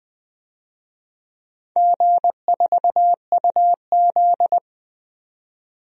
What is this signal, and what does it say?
Morse code, heard as a keyed tone, about 20 wpm, 700 Hz. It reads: G4UZ
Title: DXCC-100 Call Signs - 4 Characters - Repeated 2x - 20wpm